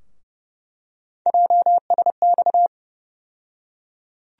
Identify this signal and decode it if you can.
Morse code, heard as a keyed tone, about 30 wpm, 700 Hz. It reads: JSX
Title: USA Call Sign Postfixes - 3 Letters 30wpm